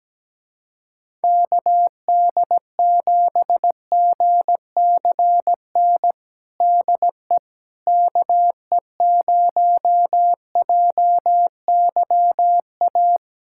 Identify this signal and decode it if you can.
Morse code, heard as a keyed tone, about 17 wpm, 700 Hz. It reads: KD7GCN DE KE0JYA
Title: USA Callsign DE Callsign 17wpm